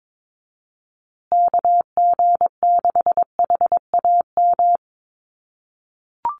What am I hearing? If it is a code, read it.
Morse code, heard as a keyed tone, about 22 wpm, 700 Hz. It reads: KG6HAM
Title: USA Call Signs Repeated 2x 22wpm